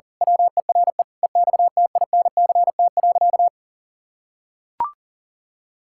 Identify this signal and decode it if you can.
Morse code, heard as a keyed tone, about 40 wpm, 700 Hz. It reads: WERE EXTINCT.
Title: Pangram Challenge v2 - 40wpm